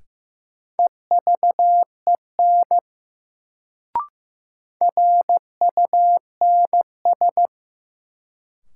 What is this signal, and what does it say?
Morse code, heard as a keyed tone, about 15 wpm, 700 Hz. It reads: EVEN RUNS
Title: Sentences from Top 600 Words - New Word Drill - 15wpm